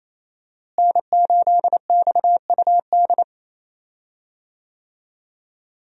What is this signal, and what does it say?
Morse code, heard as a keyed tone, about 28 wpm, 700 Hz. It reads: N8XUD
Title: USA Call Signs - 5 Characters - Repeated 2x - 28wpm